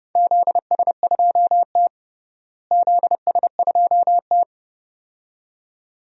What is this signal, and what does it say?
Morse code, heard as a keyed tone, about 32 wpm, 700 Hz. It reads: ZS2T ZS2T
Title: DXCC-100 Call Signs - 4 Characters - Repeated 2x - 30wpm